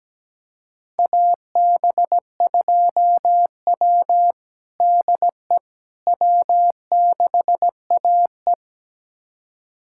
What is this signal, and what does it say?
Morse code, heard as a keyed tone, about 17 wpm, 700 Hz. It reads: AB2W DE W6AE
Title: USA Callsign DE Callsign - 4 Characters - 17wpm